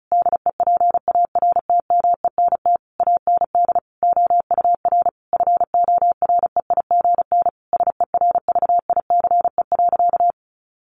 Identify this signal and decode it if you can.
Morse code, heard as a keyed tone, about 34 wpm, 700 Hz. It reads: DEPARTMENT AND OUR FOREIGN SERVICE.
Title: Sentences from Top 1400 Words 35wpm